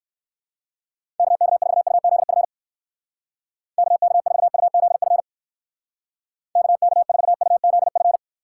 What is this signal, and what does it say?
Morse code, heard as a keyed tone, about 57 wpm, 700 Hz. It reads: KK4UZF KK4UZF KK4UZF